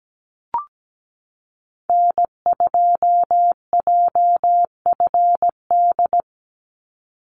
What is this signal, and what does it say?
Morse code, heard as a keyed tone, about 17 wpm, 700 Hz. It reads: N2JFD
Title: Call Signs Repeated 3x - 17wpm